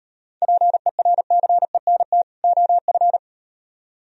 Morse code, heard as a keyed tone, about 36 wpm, 700 Hz. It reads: PERCENT OF